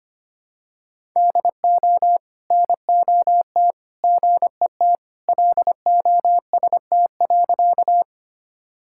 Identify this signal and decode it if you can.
Morse code, heard as a keyed tone, about 25 wpm, 700 Hz. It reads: DO NOT GET LOST.
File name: practice / 25wpm / Sentences with 3-Word Phrases - 25wpm